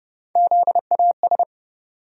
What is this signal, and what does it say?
Morse code, heard as a keyed tone, about 31 wpm, 700 Hz. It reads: ZAS
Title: USA Call Sign Postfixes - 3 Letters 30wpm